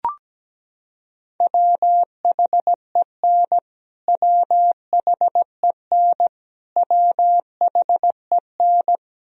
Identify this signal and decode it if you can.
Morse code, heard as a keyed tone, about 17 wpm, 700 Hz. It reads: WHEN WHEN WHEN
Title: Top 100 Words Repeated 17wpm